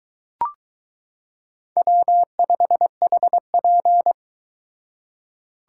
Morse code, heard as a keyed tone, about 22 wpm, 700 Hz. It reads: W5HP